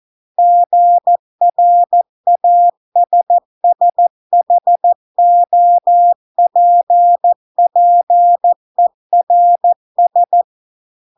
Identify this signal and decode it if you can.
Morse code, heard as a keyed tone, about 14 wpm, 700 Hz. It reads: GRASSHOPPERS